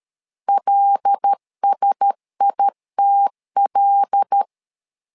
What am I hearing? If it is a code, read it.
Morse code, heard as a keyed tone, about 13 wpm, 800 Hz. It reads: LSITL
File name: LEZIONI / 5 LBVC / 5 L 12wpm 1913